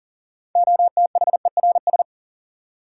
Morse code, heard as a keyed tone, about 38 wpm, 700 Hz. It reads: OTHERS
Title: Sentences from Top 1500 Words - New Word Drill Alt - 40wpm